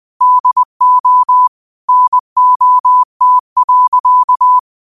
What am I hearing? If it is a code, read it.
Morse code, heard as a keyed tone, about 20 wpm, 1000 Hz. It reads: DO NOT.
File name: qotdfiles / 20wpm